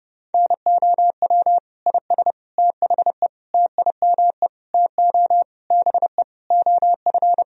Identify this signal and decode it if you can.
Morse code, heard as a keyed tone, about 32 wpm, 700 Hz. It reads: NOW IS THE TIME TO BE OF
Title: Sentences from Top 1000 Words - Review Part 4 - 30wpm